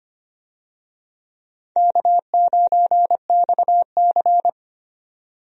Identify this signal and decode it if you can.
Morse code, heard as a keyed tone, about 25 wpm, 700 Hz. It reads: K9XC
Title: USA Call Signs - 4 Characters - 25wpm